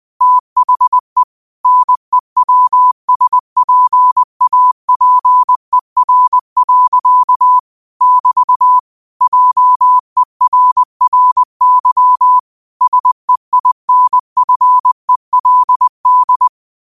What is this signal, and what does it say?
Morse code, heard as a keyed tone, about 20 wpm, 1000 Hz. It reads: THE NEWSPAPER. <BT> JERRY SEINFELD
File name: qotdfiles / 20wpm